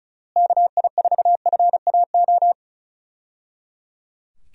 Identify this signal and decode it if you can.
Morse code, heard as a keyed tone, about 34 wpm, 700 Hz. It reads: KI4FAO
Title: USA Call Signs - 6 Characters - Repeated 3x - 35wpm